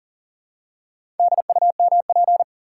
Morse code, heard as a keyed tone, about 38 wpm, 700 Hz. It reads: DUMP
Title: Four Letter Words - 40wpm